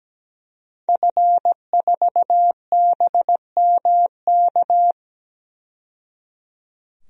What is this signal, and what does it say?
Morse code, heard as a keyed tone, about 17 wpm, 700 Hz. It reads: F4BMK
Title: DXCC-100 Call Signs - 5 Characters - 17wpm